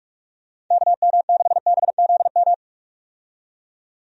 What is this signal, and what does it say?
Morse code, heard as a keyed tone, about 46 wpm, 700 Hz. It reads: KM6BZM
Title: USA Call Signs - 6 Characters - 45wpm